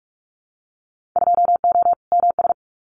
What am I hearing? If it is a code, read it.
Morse code, heard as a keyed tone, about 44 wpm, 700 Hz. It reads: 2O MS
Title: Winter Field Day Exchanges 45wpm